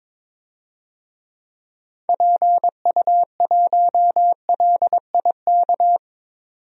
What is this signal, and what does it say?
Morse code, heard as a keyed tone, about 22 wpm, 700 Hz. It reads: PU1LIK
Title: DXCC-100 Call Signs - 6 Characters - 22wpm